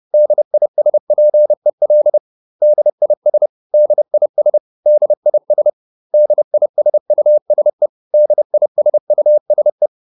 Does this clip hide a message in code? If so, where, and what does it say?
Yes: Morse code from 0.1 to 9.9 s, heard as a keyed tone, about 30 wpm, 600 Hz. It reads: DISPEL DIS DIS DIS DISUSE DISUSE